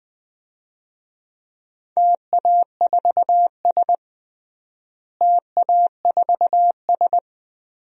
Morse code, heard as a keyed tone, about 21 wpm, 700 Hz. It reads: TA4S TA4S
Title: DXCC-100 Call Signs - 4 Characters - Repeated 3x - 20wpm